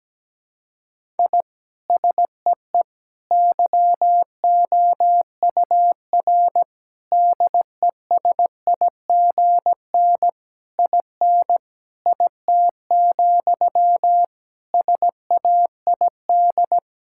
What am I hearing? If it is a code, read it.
Morse code, heard as a keyed tone, about 17 wpm, 700 Hz. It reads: I SEE YOUR DESIGN IN IT, SAID
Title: Sentences from Top 1000 Words - Review Part 5 - 17wpm